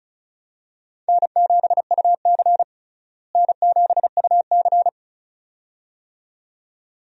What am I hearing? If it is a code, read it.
Morse code, heard as a keyed tone, about 36 wpm, 700 Hz. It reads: N7UC N7UC
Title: USA Call Signs - 4 Characters - Repeated 2x - 35wpm